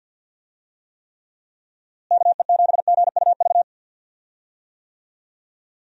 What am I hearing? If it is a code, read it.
Morse code, heard as a keyed tone, about 51 wpm, 700 Hz. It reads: KE7GUV
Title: USA Call Signs - 6 Characters - Repeated 2x - 50wpm